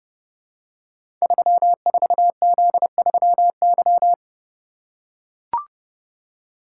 Morse code, heard as a keyed tone, about 30 wpm, 700 Hz. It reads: 34Z3Y
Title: Code Groups - 5 Letter:Numbers - 30wpm